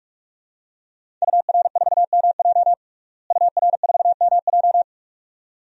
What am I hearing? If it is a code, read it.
Morse code, heard as a keyed tone, about 44 wpm, 700 Hz. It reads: UR4MJ UR4MJ
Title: DXCC-100 Call Signs - 5 Characters - Repeated 2x - 45wpm